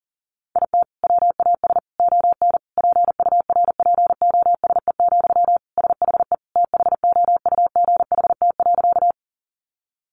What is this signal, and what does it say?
Morse code, heard as a keyed tone, about 40 wpm, 700 Hz. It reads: IT WAS ON PURPOSE, SHE THOUGHT.
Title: Sentences from Top 800 Words 40wpm